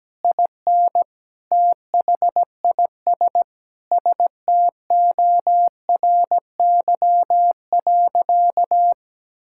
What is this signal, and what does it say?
Morse code, heard as a keyed tone, about 17 wpm, 700 Hz. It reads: IN THIS STORY.
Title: Sentences from Top 1600 Words 17wpm